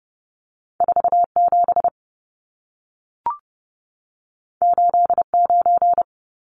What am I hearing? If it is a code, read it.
Morse code, heard as a keyed tone, about 30 wpm, 700 Hz. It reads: 47 89
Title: Numbers - 2-Digit 30wpm